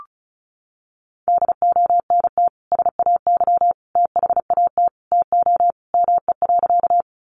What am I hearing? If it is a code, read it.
Morse code, heard as a keyed tone, about 34 wpm, 700 Hz. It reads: DONT SAY THAT TO ME.
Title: Sentences from Top 300 Words 35wpm